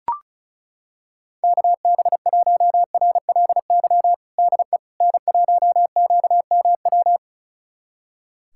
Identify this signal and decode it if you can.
Morse code, heard as a keyed tone, about 36 wpm, 690 Hz. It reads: KB1RLY DE N1QMW
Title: USA Callsign DE Callsign 35wpm